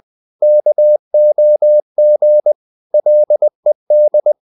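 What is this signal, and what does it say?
Morse code, heard as a keyed tone, about 20 wpm, 600 Hz. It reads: KOG LED
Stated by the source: Morse code practice of short Danish words at 20 wpm, from CW med Gnister.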